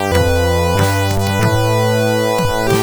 There is no code.